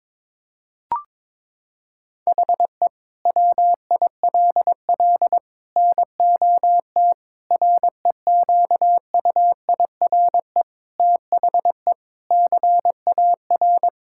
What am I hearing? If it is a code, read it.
Morse code, heard as a keyed tone, about 22 wpm, 700 Hz. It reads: HE WILL NOT REQUIRE THE CAR
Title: Sentences from Top 1000 Words - Review Part 6 - 22wpm